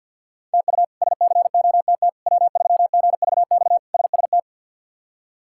Morse code, heard as a keyed tone, about 46 wpm, 700 Hz. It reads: TU SCOTT W3GVX SST
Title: K1USN SST QSO Practice - 50wpm